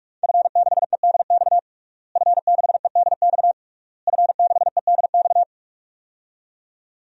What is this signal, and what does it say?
Morse code, heard as a keyed tone, about 42 wpm, 700 Hz. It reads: F6EDX F6EDX F6EDX
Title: DXCC-100 Call Signs - 5 Characters - Repeated 3x - 45wpm